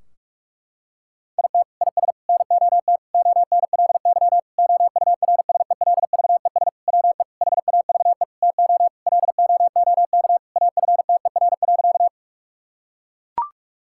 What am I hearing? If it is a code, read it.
Morse code, heard as a keyed tone, about 41 wpm, 700 Hz. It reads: IT IS NOT ONLY OURSELVES WE HAVE TO LOOK AFTER.
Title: Sentences from Top 1600 Words 45wpm